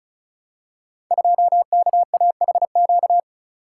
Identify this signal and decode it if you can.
Morse code, heard as a keyed tone, about 36 wpm, 700 Hz. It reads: 2KAHQ
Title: Code Groups - 5 Letter:Numbers - 35wpm